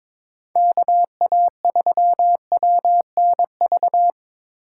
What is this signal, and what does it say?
Morse code, heard as a keyed tone, about 25 wpm, 700 Hz. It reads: KA3WNV
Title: Speed Racing - USA 2x3 Call Signs 22wpm